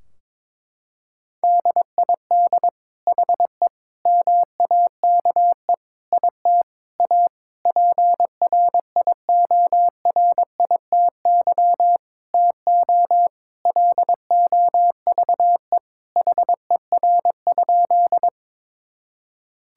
Morse code, heard as a keyed tone, about 22 wpm, 700 Hz. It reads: DID HE MAKE IT A PRIORITY TO LOVE HER?
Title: Sentences from Top 1600 Words 22wpm